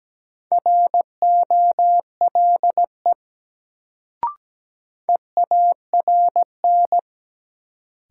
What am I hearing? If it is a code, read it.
Morse code, heard as a keyed tone, about 17 wpm, 700 Hz. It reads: ROLE EARN
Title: Sentences from Top 1500 Words - New Word Drill Alt - 17wpm